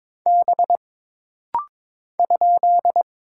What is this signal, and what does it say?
Morse code, heard as a keyed tone, about 22 wpm, 700 Hz. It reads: B ?
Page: This rapid-fire timing is geared to push your ICR (Instant Character Recognition) skills.